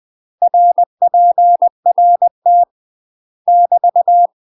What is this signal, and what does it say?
Morse code, heard as a keyed tone, about 20 wpm, 700 Hz. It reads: RPRT <BT>
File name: UK MASTER QSOs TRAINING/20WPM